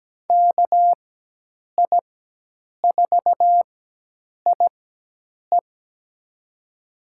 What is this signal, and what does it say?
Morse code, heard as a keyed tone, about 17 wpm, spaced as for 8 wpm, 700 Hz. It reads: KI4IE
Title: USA Call Signs - 5 Characters - 4x Character Spacing - 17wpm